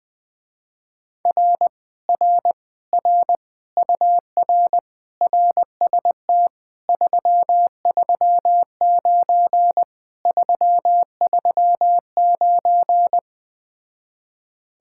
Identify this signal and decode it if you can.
Morse code, heard as a keyed tone, about 20 wpm, 700 Hz. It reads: R R R UR RST 339 339
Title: Speed-Racing Common QSO Phrases - 17wpm